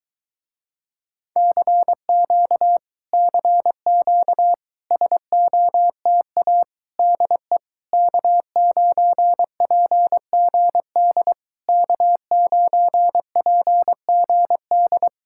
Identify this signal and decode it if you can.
Morse code, heard as a keyed tone, about 22 wpm, 700 Hz. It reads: CQ CQ SOTA DE K9PGD K9PGD